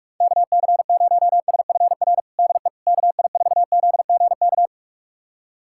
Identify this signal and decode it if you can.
Morse code, heard as a keyed tone, about 44 wpm, 700 Hz. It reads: KC0SFR DE KI4ZGK
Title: USA Callsign DE Callsign - 6 Characters - 45wpm